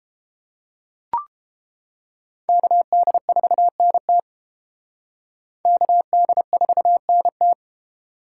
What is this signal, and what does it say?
Morse code, heard as a keyed tone, about 30 wpm, 700 Hz. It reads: KD4NT KD4NT